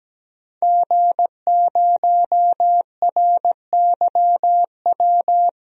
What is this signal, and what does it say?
Morse code, heard as a keyed tone, about 17 wpm, 700 Hz. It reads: G0RYW
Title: DXCC-100 Call Signs - 5 Characters - 17wpm